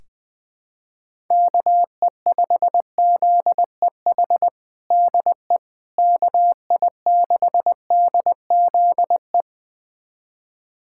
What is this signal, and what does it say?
Morse code, heard as a keyed tone, about 20 wpm, 700 Hz. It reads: KE5ZEH DE KI6DZE
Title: USA Callsign DE Callsign 20wpm